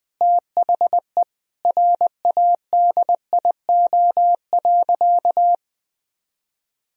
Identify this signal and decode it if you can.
Morse code, heard as a keyed tone, about 20 wpm, 700 Hz. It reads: THE RADIO.
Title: Sentences from Top 1000 Words - Review Part 3 - 20wpm